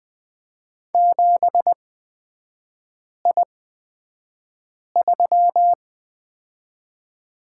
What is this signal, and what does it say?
Morse code, heard as a keyed tone, about 20 wpm, spaced as for 5 wpm, 700 Hz. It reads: 7I3
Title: 255 Lesson 41e - Period - Three Characters 20wpm